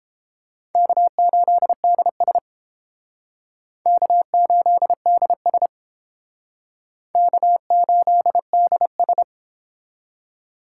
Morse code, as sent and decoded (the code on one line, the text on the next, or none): -.- ---.. -.. ... / -.- ---.. -.. ... / -.- ---.. -.. ...
K8DS K8DS K8DS